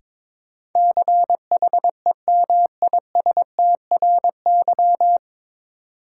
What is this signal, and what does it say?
Morse code, heard as a keyed tone, about 22 wpm, 700 Hz. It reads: CHEMISTRY
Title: Sentences from Top 1000 Words - Review Part 6 - New Word Drill - 22wpm